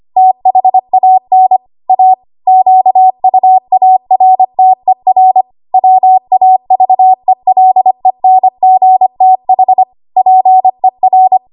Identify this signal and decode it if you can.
Morse code, heard as a keyed tone, about 25 wpm, 750 Hz. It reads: THAN A QUARTER WAVELENGTH PER
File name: Archive / 25 WPM / 25WPM